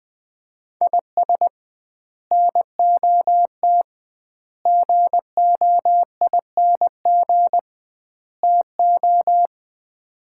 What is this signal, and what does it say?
Morse code, heard as a keyed tone, about 20 wpm, 700 Hz. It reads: IS NOT GOING TO
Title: 2x Word-Spacing - Sets of 4 Words - 20wpm